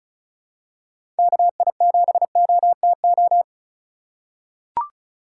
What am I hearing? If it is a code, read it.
Morse code, heard as a keyed tone, about 36 wpm, 700 Hz. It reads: KI7OTO
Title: USA Call Signs - 6 Characters - 4x Character Spacing - 35wpm